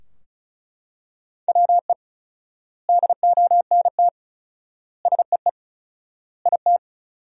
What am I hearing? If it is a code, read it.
Morse code, heard as a keyed tone, about 34 wpm, 700 Hz. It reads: WE DONT SEE IT
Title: 4x Word-Spacing - Sentences from Top 300 Words - 35wpm